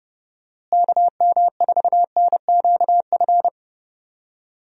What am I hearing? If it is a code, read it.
Morse code, heard as a keyed tone, about 29 wpm, 700 Hz. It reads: KM4NQF
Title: USA Call Signs - 6 Characters - Repeated 3x - 30wpm